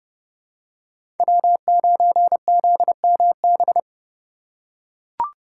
Morse code, heard as a keyed tone, about 30 wpm, 700 Hz. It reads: W9ZMB
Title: USA Call Signs - 5 Characters - 3x Character Spacing - 30wpm